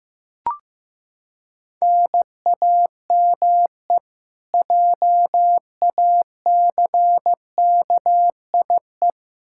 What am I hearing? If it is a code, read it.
Morse code, heard as a keyed tone, about 16 wpm, 700 Hz. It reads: NAME JACKIE
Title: Practice QSO - Exchange 1 - Beginning - 15wpm